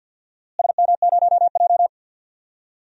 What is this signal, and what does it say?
Morse code, heard as a keyed tone, about 52 wpm, 700 Hz. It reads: SM0J
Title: DXCC-100 Call Signs - 4 Characters - Repeated 3x - 50wpm